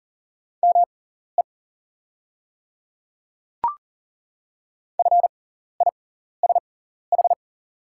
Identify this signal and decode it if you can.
Morse code, heard as a keyed tone, about 39 wpm, spaced as for 14 wpm, 690 Hz. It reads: ME FISH
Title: 6x Character Spacing - Top 300 Words - 40wpm